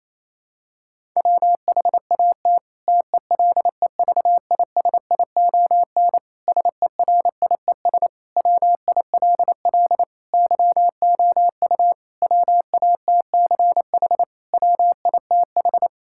Morse code, heard as a keyed tone, about 28 wpm, 700 Hz. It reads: WHAT TELEVISION SERIES WILL YOU WATCH WITH
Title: Sentences from Top 1000 Words - Review Part 5 - 28wpm